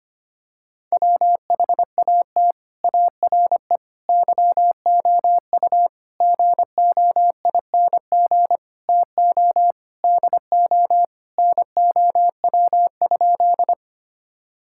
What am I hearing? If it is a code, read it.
Morse code, heard as a keyed tone, about 25 wpm, 700 Hz. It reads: WHAT ARE YOU GOING TO DO NOW?
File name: practice / 25wpm / Sentences with 3-Word Phrases - 25wpm